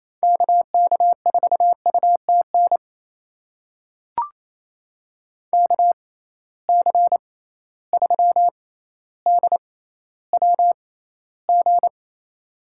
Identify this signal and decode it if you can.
Morse code, heard as a keyed tone, about 28 wpm, spaced as for 14 wpm, 700 Hz. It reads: KK4UTN KC3DWG
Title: USA Call Signs - 6 Characters - 6x Character Spacing - 28wpm